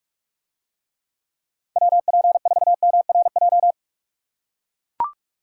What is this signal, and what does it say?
Morse code, heard as a keyed tone, about 42 wpm, 700 Hz. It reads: WP4MRJ